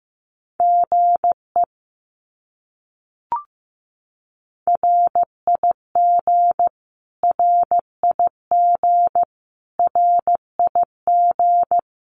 Morse code, heard as a keyed tone, about 15 wpm, 700 Hz. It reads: GE RIG RIG RIG
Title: Abbreviations Repeated 3x - 15wpm